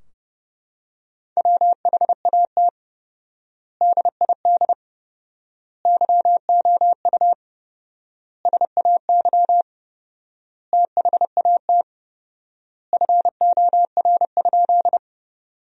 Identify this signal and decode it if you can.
Morse code, heard as a keyed tone, about 32 wpm, 700 Hz. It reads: WHAT DID YOU SAY THAT FOR?
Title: 4x Word-Spacing - Sentences from Top 200 Words - 30wpm